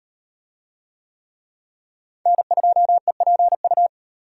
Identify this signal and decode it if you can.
Morse code, heard as a keyed tone, about 38 wpm, 700 Hz. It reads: N2EPU